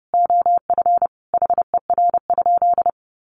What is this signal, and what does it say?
Morse code, heard as a keyed tone, about 30 wpm, 700 Hz. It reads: OF HER?
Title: Sentences from Top 200 Words 30wpm